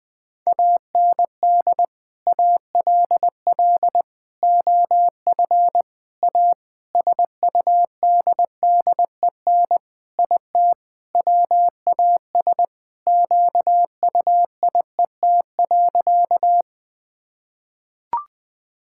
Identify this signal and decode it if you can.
Morse code, heard as a keyed tone, about 20 wpm, 700 Hz. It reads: AND ALL OF A SUDDEN IT WAS QUIET.
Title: Sentences with 5-Word Phrases - Spoken Beginning - 20wpm